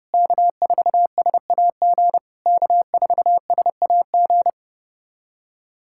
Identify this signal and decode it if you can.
Morse code, heard as a keyed tone, about 28 wpm, 660 Hz. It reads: K4SAG K4SAG
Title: SOTA QSO Practice 30wpm